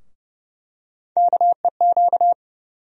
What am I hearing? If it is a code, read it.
Morse code, heard as a keyed tone, about 31 wpm, 700 Hz. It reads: KEQ
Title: USA Call Sign Postfixes - 3 Letters 30wpm